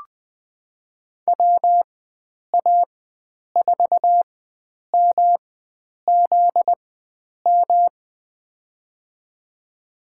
Morse code, heard as a keyed tone, about 20 wpm, spaced as for 9 wpm, 700 Hz. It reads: WA4MZM
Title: USA Call Signs - 6 Characters - 4x Character Spacing - 20wpm